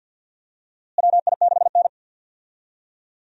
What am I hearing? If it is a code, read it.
Morse code, heard as a keyed tone, about 53 wpm, 700 Hz. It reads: WI6N
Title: USA Call Signs - 4 Characters - Repeated 3x - 50wpm